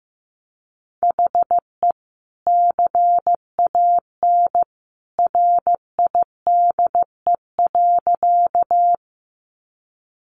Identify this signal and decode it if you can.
Morse code, heard as a keyed tone, about 15 wpm, 700 Hz. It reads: HE CAN RIDE.
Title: Sentences from Top 1200 Words 15wpm